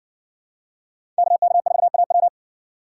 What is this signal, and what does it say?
Morse code, heard as a keyed tone, about 54 wpm, 690 Hz. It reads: KK4AW